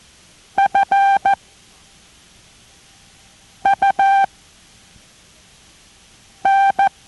Morse code, heard as a keyed tone, about 14 wpm, spaced as for 4 wpm, 790 Hz. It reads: FUN